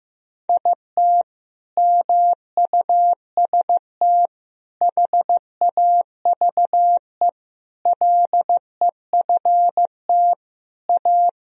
Morse code, heard as a keyed tone, about 15 wpm, 700 Hz. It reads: IT MUST HAVE LEFT A
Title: Sentences from Top 1000 Words - Review Part 1 - 15wpm